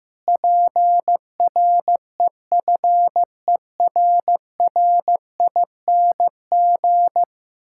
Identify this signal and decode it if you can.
Morse code, heard as a keyed tone, about 15 wpm, 700 Hz. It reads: PREFERRING
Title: Pangram Challenge v2 - 15wpm